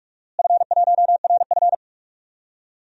Morse code, heard as a keyed tone, about 46 wpm, 700 Hz. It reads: F1RF
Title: DXCC-100 Call Signs - 4 Characters - Repeated 3x - 45wpm